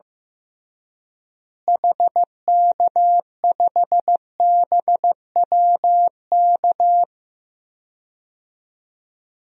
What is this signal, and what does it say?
Morse code, heard as a keyed tone, about 15 wpm, 700 Hz. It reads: HK5BWK
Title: DXCC-100 Call Signs - 6 Characters - Repeated 3x - 15wpm